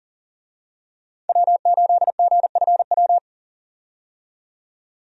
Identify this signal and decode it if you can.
Morse code, heard as a keyed tone, about 40 wpm, 700 Hz. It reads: W8GFW
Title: USA Call Signs - 5 Characters - 40wpm